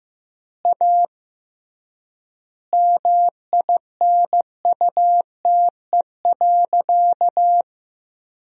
Morse code, heard as a keyed tone, about 15 wpm, 700 Hz. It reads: A MINUTE.
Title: 3x Word-Spacing - Sentences from Top 500 Words - 15wpm